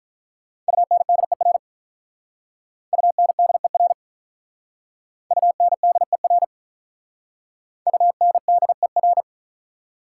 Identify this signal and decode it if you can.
Morse code, heard as a keyed tone, about 44 wpm, 700 Hz. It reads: UNDER UNDER UNDER UNDER